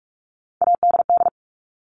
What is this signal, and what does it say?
Morse code, heard as a keyed tone, about 44 wpm, 700 Hz. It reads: ADD
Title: ED Suffix - Top 100 Words with Base Words 45wpm